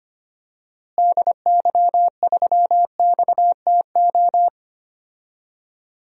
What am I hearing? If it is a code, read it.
Morse code, heard as a keyed tone, about 25 wpm, 700 Hz. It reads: DY3XTO
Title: DXCC-100 Call Signs - 6 Characters - Repeated 3x - 25wpm